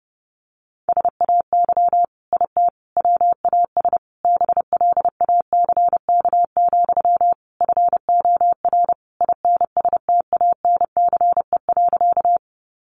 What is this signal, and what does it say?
Morse code, heard as a keyed tone, about 28 wpm, 700 Hz. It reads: SAY IT WAS BLACK, FOR INSTANCE.
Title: Sentences from Top 800 Words 30wpm